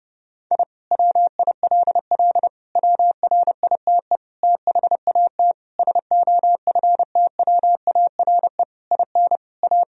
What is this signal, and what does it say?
Morse code, heard as a keyed tone, about 32 wpm, 700 Hz. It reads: I WILL WRITE THAT SOFTWARE IN A
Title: Sentences from Top 1000 Words - Review Part 3 - 30wpm